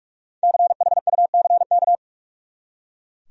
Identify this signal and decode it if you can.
Morse code, heard as a keyed tone, about 42 wpm, 700 Hz. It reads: CHUCK
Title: USA Common Names - 45wpm